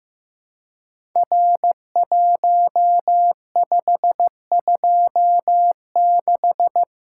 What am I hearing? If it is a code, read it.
Morse code, heard as a keyed tone, about 15 wpm, 700 Hz. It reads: R1526
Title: Code Groups - 5 Letter:Numbers - 15wpm